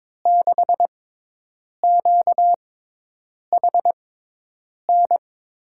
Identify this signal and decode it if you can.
Morse code, heard as a keyed tone, about 22 wpm, spaced as for 8 wpm, 700 Hz. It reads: 6QHN
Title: USA Call Signs - 6x Character Spacing - 22wpm